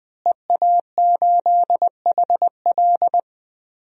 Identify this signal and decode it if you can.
Morse code, heard as a keyed tone, about 20 wpm, 700 Hz. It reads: EA8HL
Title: DXCC-100 Call Signs - 5 Characters - Repeated 2x - 20wpm